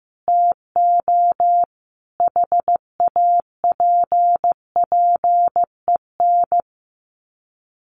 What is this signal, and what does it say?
Morse code, heard as a keyed tone, about 15 wpm, 700 Hz. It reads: TO HAPPEN
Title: Sets of 5 Words Encore 15wpm